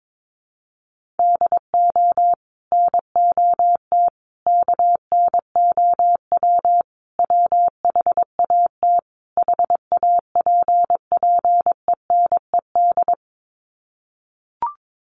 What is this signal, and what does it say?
Morse code, heard as a keyed tone, about 22 wpm, 700 Hz. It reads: DO NOT KNOW WHAT HAPPENED
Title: Sets of 5 Words Encore2 22wpm